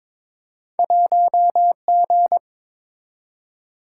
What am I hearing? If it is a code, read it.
Morse code, heard as a keyed tone, about 22 wpm, 700 Hz. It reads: 1G
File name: practice / 22wpm / Two Letter-Number Combinations - 22wpm